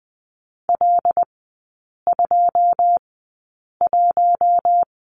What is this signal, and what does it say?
Morse code, heard as a keyed tone, about 20 wpm, spaced as for 8 wpm, 690 Hz. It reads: L21
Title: 093 Lesson 16e - Letter c - Three Characters 20wpm